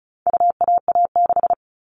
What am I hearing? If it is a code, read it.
Morse code, heard as a keyed tone, about 34 wpm, 700 Hz. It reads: UAA6
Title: All Cut Numbers - 4-Digit 35wpm